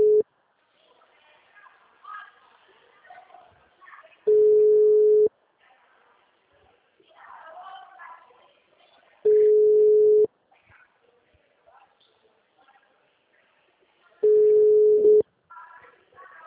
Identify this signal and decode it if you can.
Ringback tone (Europe, 425 Hz), with speech.